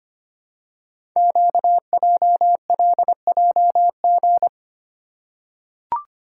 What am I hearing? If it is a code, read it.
Morse code, heard as a keyed tone, about 25 wpm, 700 Hz. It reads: QJLJG
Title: Code Groups - 5 Letters - 25wpm